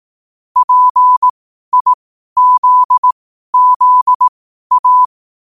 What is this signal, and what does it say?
Morse code, heard as a keyed tone, about 18 wpm, spaced as for 12 wpm, 1000 Hz. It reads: PIZZA
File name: qotdfiles / 12wpm